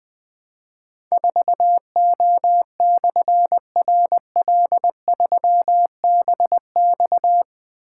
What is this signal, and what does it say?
Morse code, heard as a keyed tone, about 20 wpm, 700 Hz. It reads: 4O/RL3BX
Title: Difficult Call Signs 20wpm